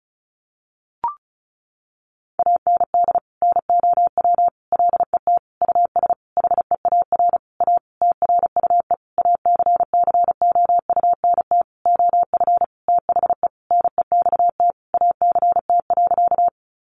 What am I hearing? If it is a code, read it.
Morse code, heard as a keyed tone, about 34 wpm, 700 Hz. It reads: AND NOW LET US HEAR A TRUE ACCOUNT OF THE NEXT ACT.
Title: Sentences from Top 900 Words 35wpm